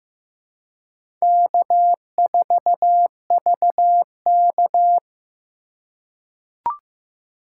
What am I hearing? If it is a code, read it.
Morse code, heard as a keyed tone, about 15 wpm, 700 Hz. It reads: K4VK
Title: USA Call Signs - 4 Characters - 2x Character Spacing - 15wpm